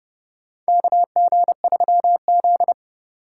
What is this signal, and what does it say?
Morse code, heard as a keyed tone, about 30 wpm, 700 Hz. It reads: KG3Z
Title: USA Call Signs - 4 Characters - 3x Character Spacing - 30wpm